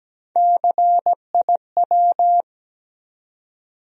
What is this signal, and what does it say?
Morse code, heard as a keyed tone, about 17 wpm, 700 Hz. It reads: CIW
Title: USA Call Sign Postfixes - 3 Letters 17wpm